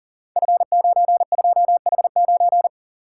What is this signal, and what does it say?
Morse code, heard as a keyed tone, about 38 wpm, 690 Hz. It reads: F92H9
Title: Code Groups - 5 Letter:Numbers - 40wpm